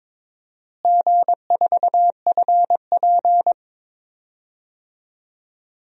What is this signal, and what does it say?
Morse code, heard as a keyed tone, about 23 wpm, 700 Hz. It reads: G4FP
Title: DXCC-100 Call Signs - 4 Characters - Repeated 2x - 22wpm